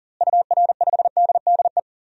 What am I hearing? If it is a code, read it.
Morse code, heard as a keyed tone, about 38 wpm, 700 Hz. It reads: UR5DDE
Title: DXCC-100 Call Signs - 6 Characters - Repeated 3x - 40wpm